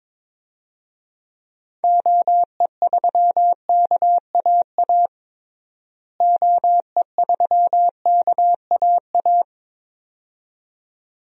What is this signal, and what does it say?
Morse code, heard as a keyed tone, about 22 wpm, 700 Hz. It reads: OE3KAA OE3KAA
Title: DXCC-100 Call Signs - 6 Characters - Repeated 2x - 22wpm